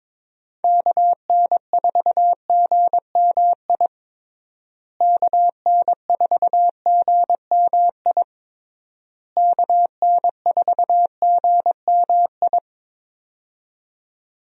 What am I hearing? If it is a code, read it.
Morse code, heard as a keyed tone, about 21 wpm, 700 Hz. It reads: KN4GMI KN4GMI KN4GMI
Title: USA Call Signs - 6 Characters - Repeated 3x - 22wpm